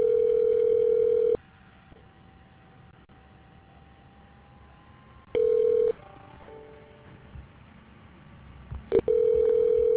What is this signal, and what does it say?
Ringback tone (North America), 440 + 480 Hz.